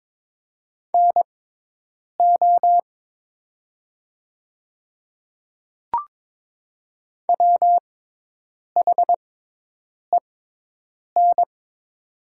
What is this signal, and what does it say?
Morse code, heard as a keyed tone, about 22 wpm, spaced as for 8 wpm, 700 Hz. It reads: NO WHEN
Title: 6x Character Spacing - Top 100 Words - 22wpm